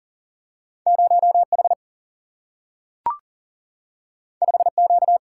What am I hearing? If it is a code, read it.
Morse code, heard as a keyed tone, about 41 wpm, 700 Hz. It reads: 0H 5Q
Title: Single-Letter Single-Number 40wpm